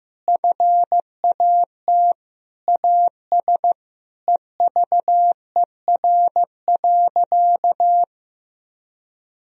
Morse code, heard as a keyed tone, about 16 wpm, 700 Hz. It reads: FAT AS EVER.
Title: Sentences from Top 1000 Words - Review Part 4 - 15wpm